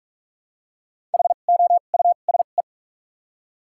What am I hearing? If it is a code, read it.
Morse code, heard as a keyed tone, about 47 wpm, spaced as for 33 wpm, 700 Hz. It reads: HOUSE
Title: 2x Character Spacing - Top 500 Words - 45wpm